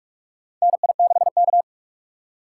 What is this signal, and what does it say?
Morse code, heard as a keyed tone, about 46 wpm, 700 Hz. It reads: NI6K
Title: USA Call Signs - 2x Character Spacing - 45wpm